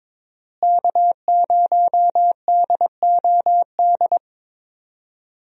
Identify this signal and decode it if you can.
Morse code, heard as a keyed tone, about 22 wpm, 700 Hz. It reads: K0DOD
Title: USA Call Signs - 5 Characters - Repeated 3x - 22wpm